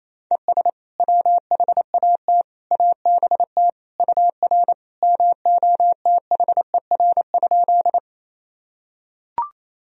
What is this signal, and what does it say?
Morse code, heard as a keyed tone, about 28 wpm, 700 Hz. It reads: ES WHAT ABT UR MOTHER?
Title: Sentences from Top 200 Words and Abbreviations 28wpm